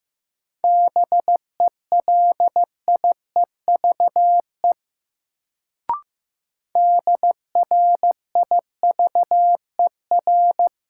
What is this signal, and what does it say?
Morse code, heard as a keyed tone, about 15 wpm, 700 Hz. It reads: BELIEVE DRIVER
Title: Sentences from Top 1000 Words - New Word Drill - 15wpm